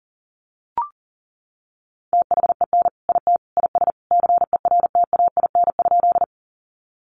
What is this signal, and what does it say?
Morse code, heard as a keyed tone, about 40 wpm, 700 Hz. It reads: THEN IT IS CERTAIN?
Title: Sentences from Top 500 Words 40wpm